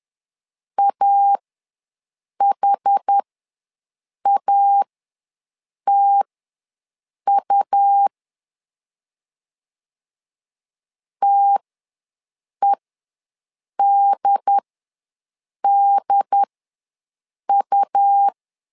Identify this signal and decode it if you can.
Morse code, heard as a keyed tone, about 10 wpm, spaced as for 6 wpm, 800 Hz. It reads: AHATU TEDDU
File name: LEZIONI / 2 UHDN / 2 UHD ws 10wpm 2058